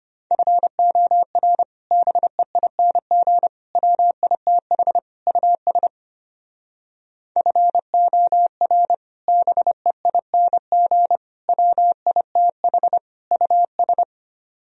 Morse code, heard as a keyed tone, about 32 wpm, 700 Hz. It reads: FOR BEING WITH US FOR BEING WITH US